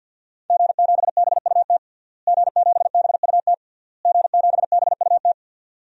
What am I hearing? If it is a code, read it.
Morse code, heard as a keyed tone, about 52 wpm, 700 Hz. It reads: G7BUT G7BUT G7BUT
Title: DXCC-100 Call Signs - 5 Characters - Repeated 3x - 50wpm